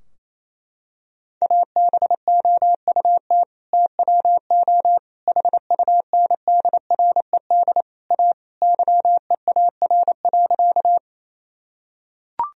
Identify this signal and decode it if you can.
Morse code, heard as a keyed tone, about 28 wpm, 700 Hz. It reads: ABOUT TWO HUNDRED A YEAR.
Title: Sentences from Top 1000 Words - Review Part 2 - 28wpm